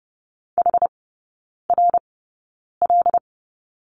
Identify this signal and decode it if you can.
Morse code, heard as a keyed tone, about 30 wpm, spaced as for 9 wpm, 700 Hz. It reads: HRL
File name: 081 Lesson 14e - Number 2 - Three Characters 30wpm